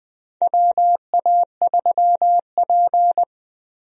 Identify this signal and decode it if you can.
Morse code, heard as a keyed tone, about 20 wpm, 700 Hz. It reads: WA3P
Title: USA Call Signs - 4 Characters - Repeated 3x - 20wpm